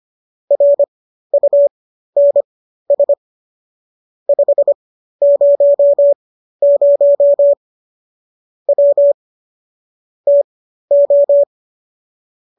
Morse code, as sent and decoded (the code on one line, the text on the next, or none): .-. ..- -. ... / ..... ----- ----- / .-- / - ---
RUNS 500 W TO